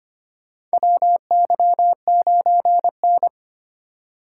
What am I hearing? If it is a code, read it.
Morse code, heard as a keyed tone, about 25 wpm, 700 Hz. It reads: WY9N
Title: USA Call Signs - 4 Characters - 2x Character Spacing - 25wpm